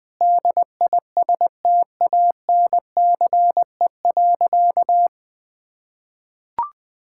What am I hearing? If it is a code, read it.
Morse code, heard as a keyed tone, about 20 wpm, 700 Hz. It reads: DISTANCE.
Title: Sentences from Top 1600 Words 20wpm